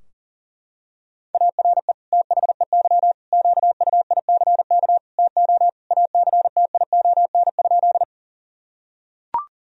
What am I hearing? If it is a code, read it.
Morse code, heard as a keyed tone, about 40 wpm, 700 Hz. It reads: ARE THEY QUICK TO ACTION?
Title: Sentences from Top 1000 Words - Review Part 4 - 40wpm